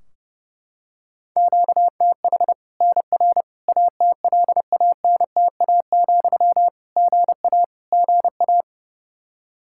Morse code, heard as a keyed tone, about 30 wpm, 700 Hz. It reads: QTH NR ATLANTA, GA GA
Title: K1USN SST QSO Practice 30wpm